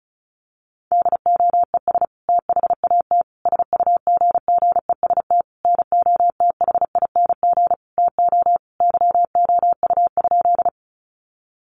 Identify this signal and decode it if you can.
Morse code, heard as a keyed tone, about 34 wpm, 700 Hz. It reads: DOES THAT SUGGEST NOTHING TO YOU?
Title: Sentences from Top 1300 Words 35wpm